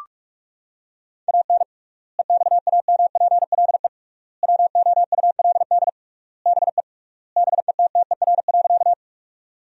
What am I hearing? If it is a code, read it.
Morse code, heard as a keyed tone, about 42 wpm, 700 Hz. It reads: AN EXAMPLE WOULD BE BETTER.
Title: 3x Word-Spacing - Sentences from Top 400 Words - 45wpm